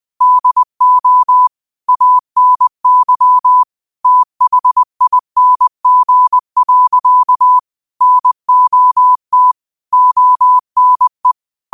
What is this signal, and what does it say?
Morse code, heard as a keyed tone, about 20 wpm, 1000 Hz. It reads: DO ANYTHING. NOT ONE